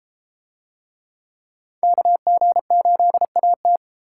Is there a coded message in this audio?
Yes: Morse code from 1.8 to 3.8 s, heard as a keyed tone, about 33 wpm, 700 Hz. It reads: KG8AT